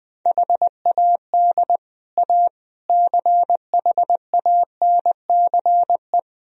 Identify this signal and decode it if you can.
Morse code, heard as a keyed tone, about 20 wpm, 700 Hz. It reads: HAD A CHANCE
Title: Speed-Racing - Sets of 4 Words 20wpm